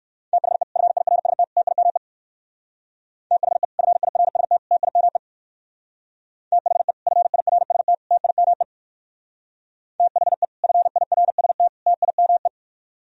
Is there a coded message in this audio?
Yes: Morse code from 0.3 to 12.5 s, heard as a keyed tone, about 58 wpm, 700 Hz. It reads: THE FIRST TIME THE FIRST TIME THE FIRST TIME THE FIRST TIME